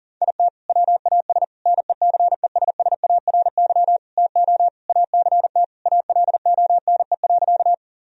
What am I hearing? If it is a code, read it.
Morse code, heard as a keyed tone, about 40 wpm, 700 Hz. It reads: IT WAS NECESSARY TO ACT ALONE.
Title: Sentences from Top 1000 Words - Review Part 1 - 40wpm